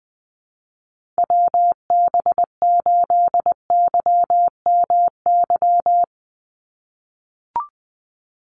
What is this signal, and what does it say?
Morse code, heard as a keyed tone, about 20 wpm, 700 Hz. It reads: WB8YMY
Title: USA Call Signs 20wpm